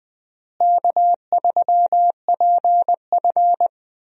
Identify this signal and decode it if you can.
Morse code, heard as a keyed tone, about 20 wpm, 700 Hz. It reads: K3PF
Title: USA Call Signs - 4 Characters - Repeated 2x - 20wpm